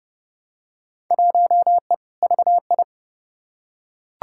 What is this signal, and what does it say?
Morse code, heard as a keyed tone, about 32 wpm, 700 Hz. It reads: 1E VI
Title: Field Day Exchanges - 30wpm